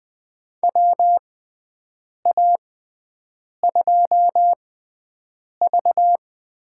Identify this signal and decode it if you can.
Morse code, heard as a keyed tone, about 20 wpm, spaced as for 7 wpm, 700 Hz. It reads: WA2V
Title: USA Call Signs - 6 Characters - 6x Character Spacing - 20wpm